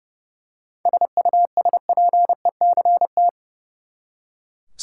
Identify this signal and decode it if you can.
Morse code, heard as a keyed tone, about 30 wpm, 700 Hz. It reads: SUSPECT
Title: Seven Letter Words - 30wpm